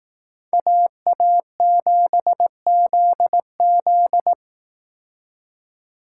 Morse code, heard as a keyed tone, about 18 wpm, 700 Hz. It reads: AA7ZZ
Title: Speed Racing - USA 2x2 Call Signs 15wpm